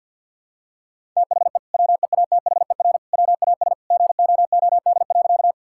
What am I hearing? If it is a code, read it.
Morse code, heard as a keyed tone, about 50 wpm, 700 Hz. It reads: THE WEATHER WAS GOOD.
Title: Sentences from Top 1000 Words - Review Part 5 - 50wpm